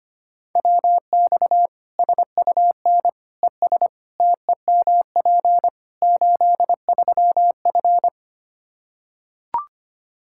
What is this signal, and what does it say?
Morse code, heard as a keyed tone, about 25 wpm, 700 Hz. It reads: WX SUN ES TEMP 83F
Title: K1USN SST QSO Practice 25wpm